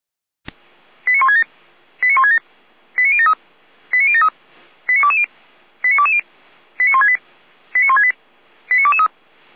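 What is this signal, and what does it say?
Five-tone selective call (ZVEI), heard as quick runs of five tones, tones of about 80 ms. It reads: 89167 89167 89073 89073 89220 89220 89168 89168 89223
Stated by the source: ZVEI five-tone call received on an SDR, from the feuerwehr-orlamuende.org (fire brigade Orlamünde) recording server.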